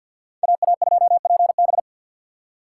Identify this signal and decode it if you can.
Morse code, heard as a keyed tone, about 48 wpm, 700 Hz. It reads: AA2PB
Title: Speed Racing - USA 2x2 Call Signs 50wpm